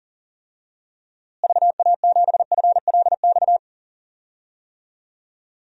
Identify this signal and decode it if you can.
Morse code, heard as a keyed tone, about 37 wpm, 700 Hz. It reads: VA7FLX
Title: DXCC-100 Call Signs - 6 Characters - Repeated 2x - 40wpm